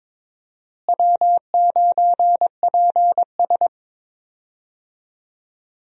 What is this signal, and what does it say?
Morse code, heard as a keyed tone, about 22 wpm, 700 Hz. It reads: W9PS
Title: USA Call Signs - 4 Characters - Repeated 3x - 22wpm